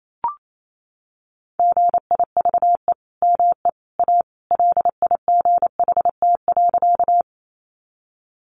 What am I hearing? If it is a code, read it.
Morse code, heard as a keyed tone, about 28 wpm, 700 Hz. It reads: GIVE ME A LIGHT.
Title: Sentences from Top 200 Words 28wpm